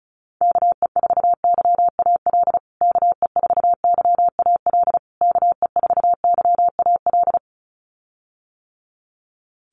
Morse code, heard as a keyed tone, about 34 wpm, 700 Hz. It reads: KE4YAL KE4YAL KE4YAL
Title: USA Call Signs Repeated 3x 35wpm